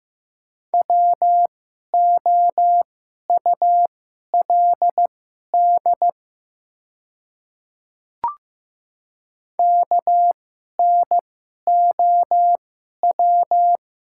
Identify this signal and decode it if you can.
Morse code, heard as a keyed tone, about 15 wpm, spaced as for 11 wpm, 700 Hz. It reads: WOULD KNOW
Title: 2x Character Spacing - Top 100 Words - 15wpm